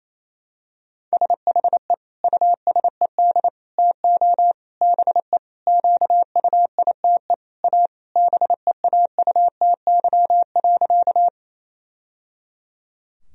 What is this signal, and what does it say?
Morse code, heard as a keyed tone, about 27 wpm, 700 Hz. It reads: SHE USED TO BE QUITE A BEAUTY.
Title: Sentences from Top 1000 Words - Review Part 2 - 28wpm